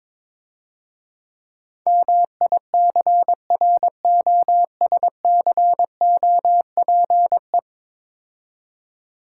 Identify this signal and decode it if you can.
Morse code, heard as a keyed tone, about 22 wpm, 700 Hz. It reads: MICROSCOPE
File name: Easy Words - Speed-Racing with Hints - 22wpm